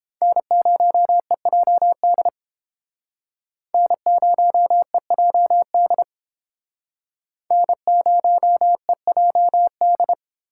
Morse code, heard as a keyed tone, about 31 wpm, 700 Hz. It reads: N0EJD N0EJD N0EJD